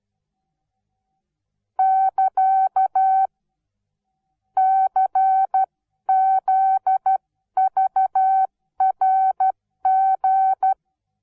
Morse code, heard as a keyed tone, about 12 wpm, 760 Hz. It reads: <KA> CZVRG
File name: Onlinelehrgang / morsen / DJ4UF Morsen17